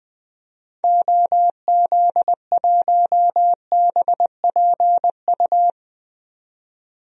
Morse code, heard as a keyed tone, about 20 wpm, 700 Hz. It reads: OZ1BPU
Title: DXCC-100 Call Signs - 6 Characters - 20wpm